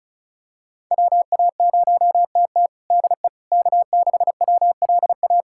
Morse code, heard as a keyed tone, about 36 wpm, 700 Hz. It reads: WA0TT DE K6WLA
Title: USA Callsign DE Callsign - 5 Characters - 35wpm